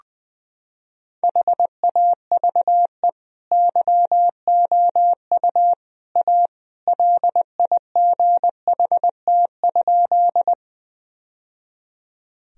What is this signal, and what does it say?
Morse code, heard as a keyed tone, about 21 wpm, 700 Hz. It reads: HAVE YOU A LIGHT?
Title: Sentences from Top 1000 Words - Review Part 1 - 20wpm